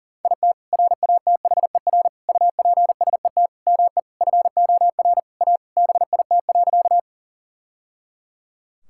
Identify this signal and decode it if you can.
Morse code, heard as a keyed tone, about 40 wpm, 700 Hz. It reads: IT RATHER UPSET ME FOR A BIT.
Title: Sentences from Top 1000 Words - Review Part 5 - 40wpm